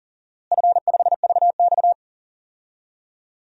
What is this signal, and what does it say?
Morse code, heard as a keyed tone, about 40 wpm, 700 Hz. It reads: F5VX
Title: DXCC-100 Call Signs - 4 Characters - Repeated 3x - 40wpm